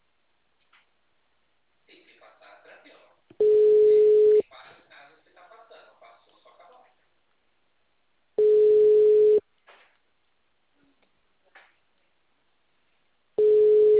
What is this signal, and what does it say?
Ringback tone (Europe, 425 Hz), with speech.
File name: out- -2038- - - .634